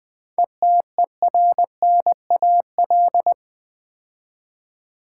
Morse code, heard as a keyed tone, about 20 wpm, 700 Hz. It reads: ETERNAL
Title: 7 Letter Words - First 3 Letters Given - 20wpm